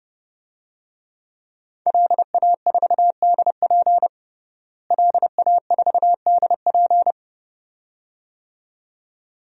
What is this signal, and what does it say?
Morse code, heard as a keyed tone, about 30 wpm, 700 Hz. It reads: LA4DP LA4DP
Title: DXCC-100 Call Signs - 5 Characters - Repeated 2x - 30wpm